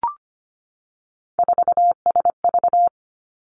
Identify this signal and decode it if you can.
Morse code, heard as a keyed tone, about 25 wpm, 700 Hz. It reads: 4SV
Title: Three Challenging Letter-Number Combinations 25wpm